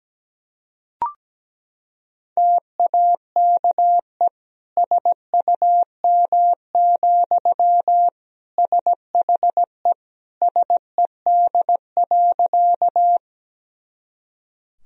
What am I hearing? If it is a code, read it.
Morse code, heard as a keyed tone, about 17 wpm, 700 Hz. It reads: TAKE SUM, SHE SED.
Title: Sentences from Top 200 Words and Abbreviations 17wpm